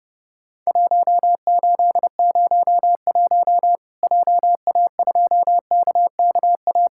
Morse code, heard as a keyed tone, about 30 wpm, 700 Hz. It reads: 1801 JA2KKA
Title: Worked All Europe DX Contest - QTC Exchange - 30wpm